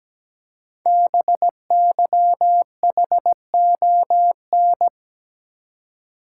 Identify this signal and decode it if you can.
Morse code, heard as a keyed tone, about 17 wpm, 700 Hz. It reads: BYHON
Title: Code Groups - 5 Letters - 17wpm